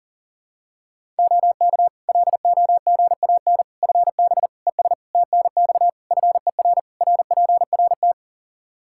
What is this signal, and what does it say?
Morse code, heard as a keyed tone, about 40 wpm, 700 Hz. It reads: OK LOGAN FB ES TNX FER RPRT
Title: Practice QSO - Exchange 2 - Info - 40wpm